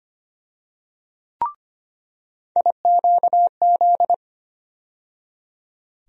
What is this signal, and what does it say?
Morse code, heard as a keyed tone, about 25 wpm, 700 Hz. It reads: IQZ
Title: USA Call Sign Postfixes - 3 Letters 25wpm